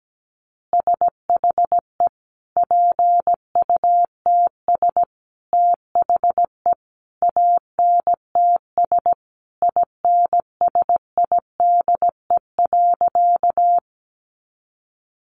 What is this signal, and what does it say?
Morse code, heard as a keyed tone, about 17 wpm, 700 Hz. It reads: SHE PUTS THE ANTS INSIDE.
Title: Easy Phrases 17wpm